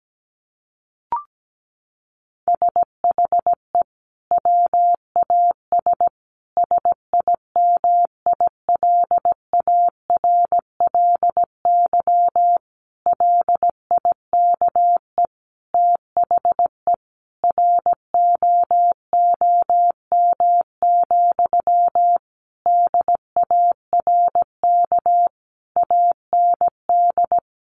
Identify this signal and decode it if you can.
Morse code, heard as a keyed tone, about 17 wpm, 700 Hz. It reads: SHE WAS SIMILARLY LIKE THE ROOM, DARK AND
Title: Sentences from Top 1400 Words 17wpm